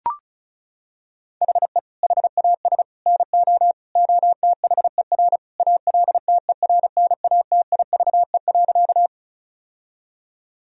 Morse code, heard as a keyed tone, about 34 wpm, 700 Hz. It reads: HE HAS NO OTHER ALTERNATIVE.
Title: Sentences from Top 1000 Words - Review Part 4 - 35wpm